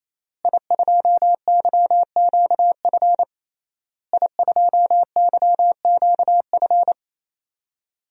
Morse code, heard as a keyed tone, about 28 wpm, 700 Hz. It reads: I2YQF I2YQF
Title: DXCC-100 Call Signs - 5 Characters - Repeated 3x - 28wpm